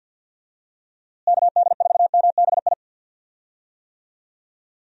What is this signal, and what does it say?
Morse code, heard as a keyed tone, about 51 wpm, 700 Hz. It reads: KD4MBI
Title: USA Call Signs - 6 Characters - Repeated 2x - 50wpm